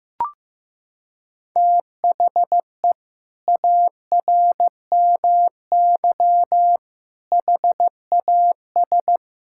Morse code, heard as a keyed tone, about 15 wpm, 700 Hz. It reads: THE ARMY HAS
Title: Sentences from Top 1000 Words - Review Part 5 - 15wpm